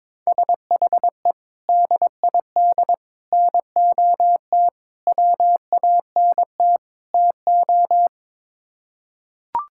Morse code, heard as a keyed tone, about 22 wpm, 700 Hz. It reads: SHE DID NOT WANT TO
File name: practice / 22wpm / Build A Phrase - 5-Word Phrases - 22wpm